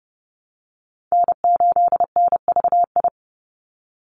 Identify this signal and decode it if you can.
Morse code, heard as a keyed tone, about 30 wpm, 700 Hz. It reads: N8NVI
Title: USA Call Signs 30wpm